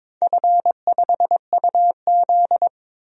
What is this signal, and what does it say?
Morse code, heard as a keyed tone, about 22 wpm, 700 Hz. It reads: F5UZ
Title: DXCC-100 Call Signs - 4 Characters - 22wpm